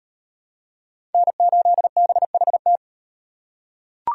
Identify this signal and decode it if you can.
Morse code, heard as a keyed tone, about 35 wpm, 700 Hz. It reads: N8BHT